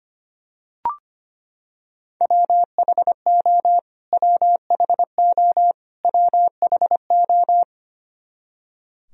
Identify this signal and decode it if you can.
Morse code, heard as a keyed tone, about 25 wpm, 700 Hz. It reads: WHO WHO WHO
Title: Top 100 Words Repeated 25wpm